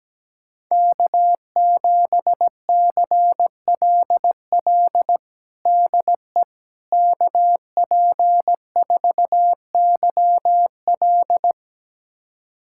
Morse code, heard as a keyed tone, about 17 wpm, 700 Hz. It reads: K7CLL DE KP4YL
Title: USA Callsign DE Callsign - 5 Characters - 17wpm